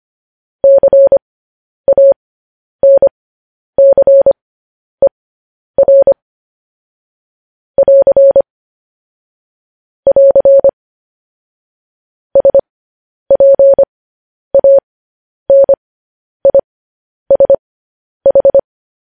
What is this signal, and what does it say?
Morse code, heard as a keyed tone, about 25 wpm, spaced as for 10 wpm, 550 Hz. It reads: CANCER <AR> <AR> SPANISH
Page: news headlines in morse code at 10 wpm